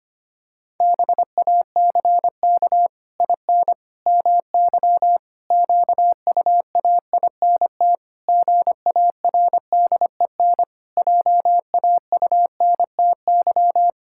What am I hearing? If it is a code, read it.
Morse code, heard as a keyed tone, about 25 wpm, 700 Hz. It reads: BACK IN MY QUAINT GARDEN JAUNTY
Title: Pangram Challenge v2 - 25wpm